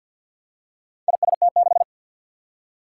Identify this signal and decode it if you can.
Morse code, heard as a keyed tone, about 48 wpm, 700 Hz. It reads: IST6
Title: Four Challenging Letter-Number Combinations 50wpm